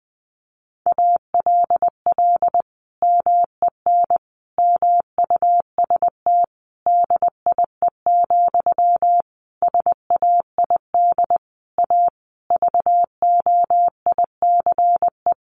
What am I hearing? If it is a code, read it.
Morse code, heard as a keyed tone, about 20 wpm, 700 Hz. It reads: ALL MEN MUST DIE, SAID A VOICE
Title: Sentences from Top 1300 Words 20wpm